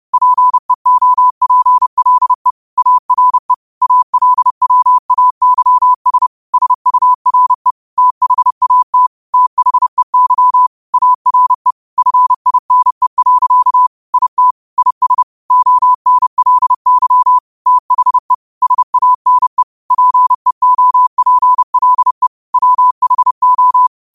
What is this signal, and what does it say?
Morse code, heard as a keyed tone, about 30 wpm, 1000 Hz. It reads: PEOPLE ARE ALWAYS SURE THAT THEY ARE FINE. IT IS ONLY THE SANE PEOPLE WHO
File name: qotdfiles / 30wpm